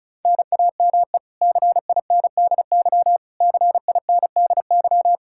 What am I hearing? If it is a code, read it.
Morse code, heard as a keyed tone, about 36 wpm, 700 Hz. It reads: NAME CINDY CINDY
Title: K1USN SST QSO Practice 35wpm